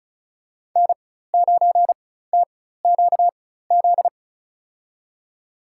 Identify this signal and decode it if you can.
Morse code, heard as a keyed tone, about 36 wpm, spaced as for 16 wpm, 700 Hz. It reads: N9TQZ
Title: USA Call Signs - 5 Characters - 4x Character Spacing - 35wpm